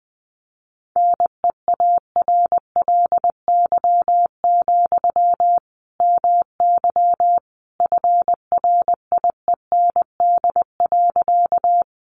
Morse code, heard as a keyed tone, about 20 wpm, 700 Hz. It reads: NEARLY, MY FRIEND.
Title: Sentences from Top 1300 Words 20wpm